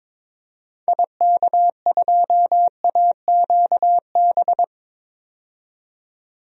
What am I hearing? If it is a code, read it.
Morse code, heard as a keyed tone, about 22 wpm, 700 Hz. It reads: IK2AQB
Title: DXCC-100 Call Signs - 6 Characters - 22wpm